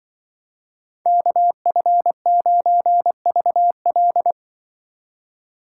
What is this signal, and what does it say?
Morse code, heard as a keyed tone, about 24 wpm, 700 Hz. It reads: KF9VL
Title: Speed Racing - USA 2x2 Call Signs 20wpm